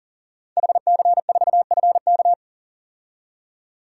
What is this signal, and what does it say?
Morse code, heard as a keyed tone, about 38 wpm, 700 Hz. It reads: HC4FK
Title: DXCC-100 Call Signs - 5 Characters - 40wpm